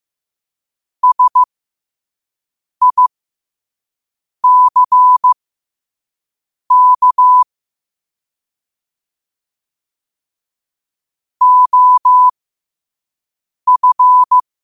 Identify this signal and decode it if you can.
Morse code, heard as a keyed tone, about 15 wpm, spaced as for 5 wpm, 1000 Hz. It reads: SICK OF
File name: qotdfiles / 5wpm